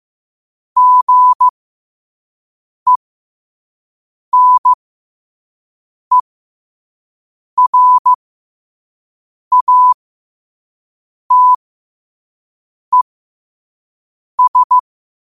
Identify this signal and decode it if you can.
Morse code, heard as a keyed tone, about 15 wpm, spaced as for 5 wpm, 1000 Hz. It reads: GENERATES
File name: qotdfiles / 5wpm